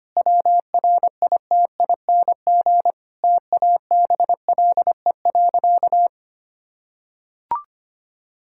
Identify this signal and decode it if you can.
Morse code, heard as a keyed tone, about 25 wpm, 700 Hz. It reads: WRITING TABLE.
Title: Sentences from Top 1000 Words - Review Part 5 - 25wpm